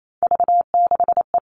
Morse code, heard as a keyed tone, about 28 wpm, 700 Hz. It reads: V6E
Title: All Cut Numbers - 3-Digit 28wpm